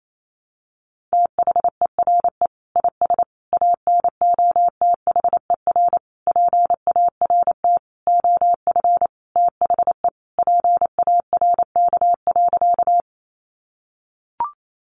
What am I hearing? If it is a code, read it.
Morse code, heard as a keyed tone, about 28 wpm, 700 Hz. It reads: THERE IS ANOTHER PART OF THE PARK.
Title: Sentences from Top 1300 Words 28wpm